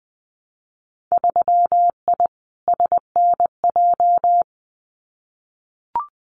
Morse code, heard as a keyed tone, about 20 wpm, 700 Hz. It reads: 3I SNJ
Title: Winter Field Day Exchanges Encore 20wpm